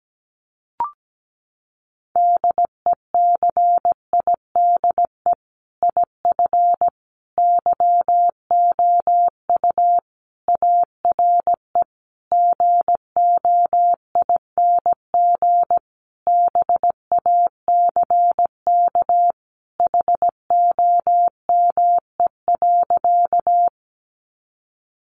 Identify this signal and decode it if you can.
Morse code, heard as a keyed tone, about 17 wpm, 700 Hz. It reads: DECIDE IF YOU ARE GOING BACK HOME.
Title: Sentences from Top 500 Words 17wpm